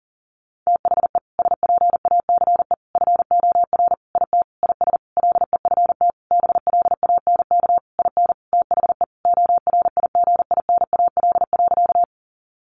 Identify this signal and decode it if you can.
Morse code, heard as a keyed tone, about 40 wpm, 700 Hz. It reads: THE SPACE FOR IT IS LEFT BLANK IN THE ORIGINAL.
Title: Sentences from Top 1500 Words 40wpm